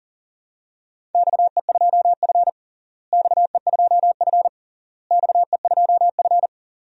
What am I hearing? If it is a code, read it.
Morse code, heard as a keyed tone, about 37 wpm, 700 Hz. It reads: XE2F XE2F XE2F
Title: DXCC-100 Call Signs - 4 Characters - Repeated 3x - 40wpm